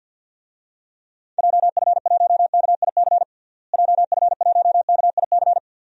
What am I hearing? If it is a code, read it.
Morse code, heard as a keyed tone, about 51 wpm, 700 Hz. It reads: JF1KIC JF1KIC
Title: DXCC-100 Call Signs - 6 Characters - Repeated 3x - 50wpm